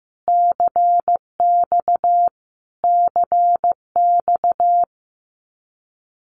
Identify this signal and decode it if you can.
Morse code, heard as a keyed tone, about 15 wpm, 700 Hz. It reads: CX CX
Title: Abbreviations Repeated 3x - 15wpm